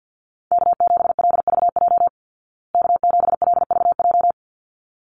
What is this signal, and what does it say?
Morse code, heard as a keyed tone, about 50 wpm, 700 Hz. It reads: K7LVJ K7LVJ
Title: Call Signs Repeated 3x - 50wpm